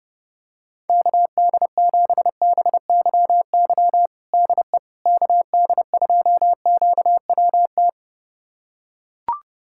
Morse code, heard as a keyed tone, about 30 wpm, 700 Hz. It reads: KD7BYY DE KD2QWT
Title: USA Callsign DE Callsign - 6 Characters - 30wpm